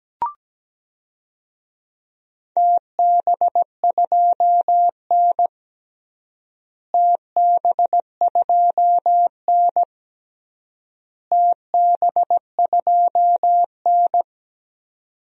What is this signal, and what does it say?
Morse code, heard as a keyed tone, about 17 wpm, 700 Hz. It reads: TB2N TB2N TB2N
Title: DXCC-100 Call Signs - 4 Characters - Repeated 3x - 17wpm